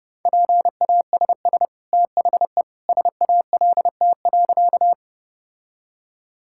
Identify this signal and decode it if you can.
Morse code, heard as a keyed tone, about 30 wpm, 700 Hz. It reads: PASS THE SALT.
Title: Sentences from Top 1000 Words - Review Part 5 - 30wpm